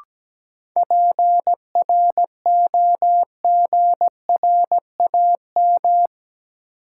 Morse code, heard as a keyed tone, about 17 wpm, 700 Hz. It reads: PROGRAM
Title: Sentences from Top 1500 Words - New Word Drill Alt - 17wpm